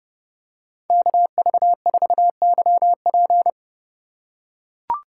Morse code, heard as a keyed tone, about 29 wpm, 700 Hz. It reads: KV4YP